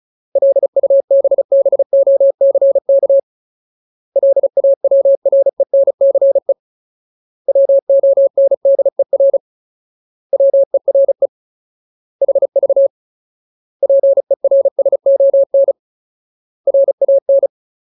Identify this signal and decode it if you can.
Morse code, heard as a keyed tone, about 35 wpm, 550 Hz. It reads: LUBBOCK LAWRENCE WONDER WERE HV PERSON RAN
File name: Ultimate Word List / 35WPM Ultimate